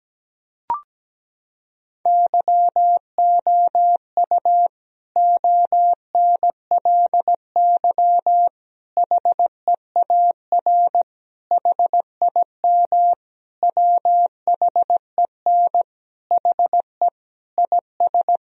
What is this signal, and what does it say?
Morse code, heard as a keyed tone, about 17 wpm, 700 Hz. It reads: YOU ONLY HEAR HIM WHEN HE IS
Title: Sentences from Top 1000 Words - Review Part 2 - 17wpm